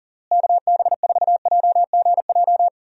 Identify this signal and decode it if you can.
Morse code, heard as a keyed tone, about 40 wpm, 700 Hz. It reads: KB4JGJ
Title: USA Call Signs - 6 Characters - Repeated 3x - 40wpm